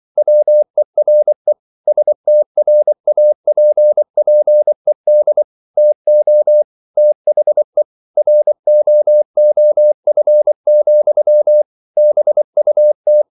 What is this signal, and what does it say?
Morse code, heard as a keyed tone, about 24 wpm, 600 Hz. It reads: WERE STRAPPED TO THE ROOF, BUT